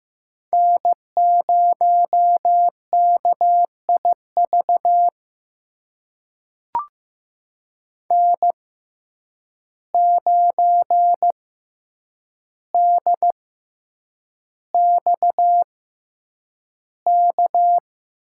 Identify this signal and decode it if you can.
Morse code, heard as a keyed tone, about 15 wpm, spaced as for 8 wpm, 700 Hz. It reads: N0KIV N9DXK
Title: USA Call Signs - 5 Characters - 6x Character Spacing - 15wpm